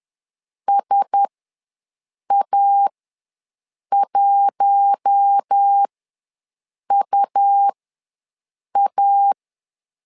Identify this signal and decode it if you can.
Morse code, heard as a keyed tone, about 10 wpm, spaced as for 6 wpm, 800 Hz. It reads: SA1UA
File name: LEZIONI / 4 Q1S ws 10wpm 2208